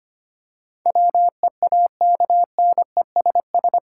Morse code, heard as a keyed tone, about 25 wpm, 700 Hz. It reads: WEAKNESS
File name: 8 Letter Words - First 3 Letters Given - 25wpm